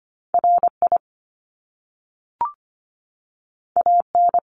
Morse code, heard as a keyed tone, about 25 wpm, 700 Hz. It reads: RI AN
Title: Top Two Letter Combinations in Words 25wpm